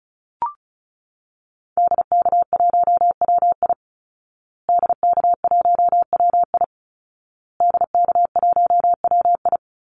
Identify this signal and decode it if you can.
Morse code, heard as a keyed tone, about 34 wpm, 700 Hz. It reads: DK1WI DK1WI DK1WI
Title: Call Signs Repeated 2x - 35wpm